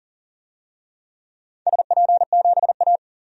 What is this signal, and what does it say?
Morse code, heard as a keyed tone, about 40 wpm, 700 Hz. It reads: SP7A
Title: DXCC-100 Call Signs - 4 Characters - 40wpm